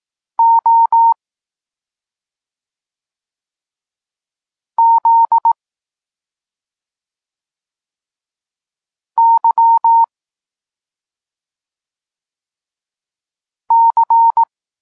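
Morse code, heard as a keyed tone, about 18 wpm, 920 Hz. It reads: O Z Y C
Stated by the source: Polish Morse lesson teaching the letter C, from SP9MOA Niepołomicki Klub Krótkofalowców.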